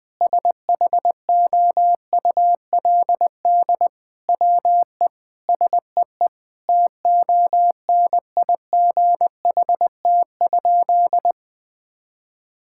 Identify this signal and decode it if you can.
Morse code, heard as a keyed tone, about 20 wpm, 700 Hz. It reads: SHOULD WE SEE TONIGHT?
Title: Sentences from Top 1000 Words - Review Part 5 - 20wpm